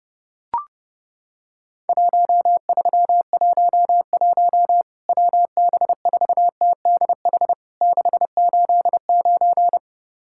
Morse code, heard as a keyed tone, about 32 wpm, 700 Hz. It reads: 1311 WB4TDH 689
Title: Worked All Europe DX Contest - QTC Exchange - 30wpm